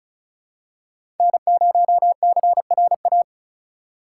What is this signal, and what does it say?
Morse code, heard as a keyed tone, about 36 wpm, 700 Hz. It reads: N0CRA
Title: USA Call Signs - 5 Characters - 3x Character Spacing - 35wpm